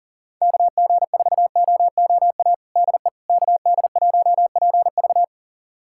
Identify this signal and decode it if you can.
Morse code, heard as a keyed tone, about 40 wpm, 700 Hz. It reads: KG4OOA DE KD1PV
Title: USA Callsign DE Callsign 40wpm